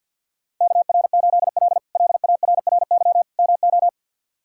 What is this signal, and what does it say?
Morse code, heard as a keyed tone, about 52 wpm, 700 Hz. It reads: KR8L LARRY MO
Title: K1USN SST QSO Practice - 50wpm